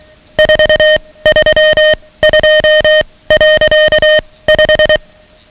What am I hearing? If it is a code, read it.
Morse code, heard as a keyed tone, about 23 wpm, spaced as for 17 wpm, 620 Hz, with noise. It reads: 432.5
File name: ham radio / CodeQuick / Code Quick 10